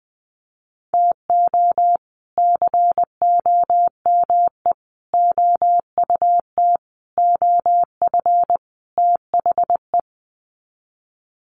Morse code, heard as a keyed tone, about 20 wpm, 700 Hz. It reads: TO COME OUT OF THE
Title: Sets of 5 Words Encore2 20wpm